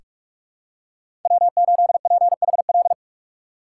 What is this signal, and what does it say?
Morse code, heard as a keyed tone, about 46 wpm, 700 Hz. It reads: W8PHL
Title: USA Call Signs - 5 Characters - 6x Character Spacing - 45wpm